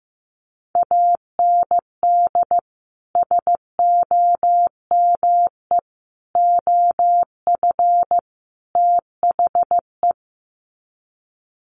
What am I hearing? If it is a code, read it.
Morse code, heard as a keyed tone, about 15 wpm, 700 Hz. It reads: AND SOME OF THE
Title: Sets of 4 Words 15wpm